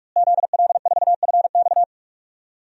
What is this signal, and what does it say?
Morse code, heard as a keyed tone, about 41 wpm, 700 Hz. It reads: ZL4FX
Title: DXCC-100 Call Signs - 5 Characters - Repeated 3x - 45wpm